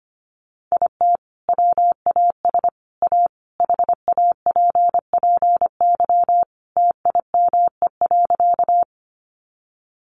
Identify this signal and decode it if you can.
Morse code, heard as a keyed tone, about 25 wpm, 700 Hz. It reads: IT WAS A HAPPY TIME.
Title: Sentences from Top 1000 Words 25wpm